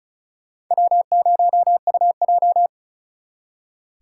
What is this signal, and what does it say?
Morse code, heard as a keyed tone, about 42 wpm, 700 Hz. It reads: W0UJ
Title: Speed Racing - USA Call Signs 35wpm